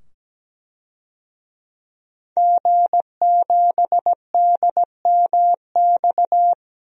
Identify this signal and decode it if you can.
Morse code, heard as a keyed tone, about 17 wpm, 700 Hz. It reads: G7DMX
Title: DXCC-100 Call Signs - 5 Characters - 17wpm